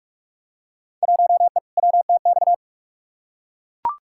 Morse code, heard as a keyed tone, about 41 wpm, 700 Hz. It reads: 1E WTX
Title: Field Day Exchanges - 45wpm